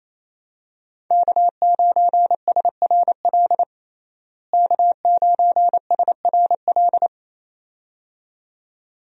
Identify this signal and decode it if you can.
Morse code, heard as a keyed tone, about 28 wpm, 700 Hz. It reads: K9SRL K9SRL
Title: USA Call Signs - 5 Characters - Repeated 2x - 28wpm